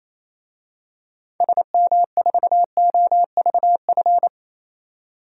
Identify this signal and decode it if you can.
Morse code, heard as a keyed tone, about 28 wpm, 700 Hz. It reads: SM4OVF
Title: DXCC-100 Call Signs - 6 Characters - 28wpm